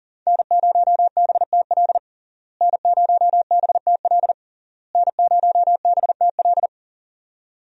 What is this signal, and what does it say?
Morse code, heard as a keyed tone, about 41 wpm, 700 Hz. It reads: N0BTL N0BTL N0BTL
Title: USA Call Signs - 5 Characters - Repeated 3x - 40wpm